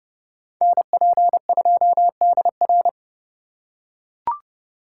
Morse code, heard as a keyed tone, about 30 wpm, 700 Hz. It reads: NP2DR
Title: USA Call Signs - 5 Characters - 3x Character Spacing - 30wpm